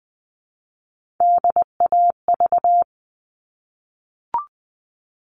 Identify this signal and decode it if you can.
Morse code, heard as a keyed tone, about 20 wpm, 700 Hz. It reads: DAV
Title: All Cut Numbers - 3-Digit 20wpm